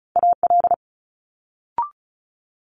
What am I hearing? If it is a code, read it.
Morse code, heard as a keyed tone, about 36 wpm, 700 Hz. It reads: AL
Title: Multi-letter Phonemes Quick Fire 35wpm